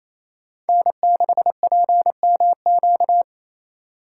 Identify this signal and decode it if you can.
Morse code, heard as a keyed tone, about 28 wpm, 700 Hz. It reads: N6PMQ
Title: USA Call Signs - 5 Characters - 6x Character Spacing - 28wpm